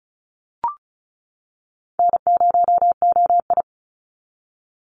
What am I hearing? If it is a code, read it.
Morse code, heard as a keyed tone, about 36 wpm, 700 Hz. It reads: N0OI
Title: NA SOTA Chaser Callsigns 35wpm